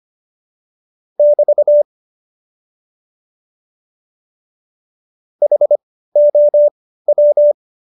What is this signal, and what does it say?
Morse code, heard as a keyed tone, about 25 wpm, spaced as for 15 wpm, 590 Hz. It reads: <BT> HOW
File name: Jokes-25-char-15-eff